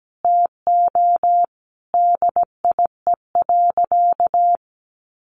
Morse code, heard as a keyed tone, about 17 wpm, 700 Hz. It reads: TO DIE.
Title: Sentences from Top 1300 Words 17wpm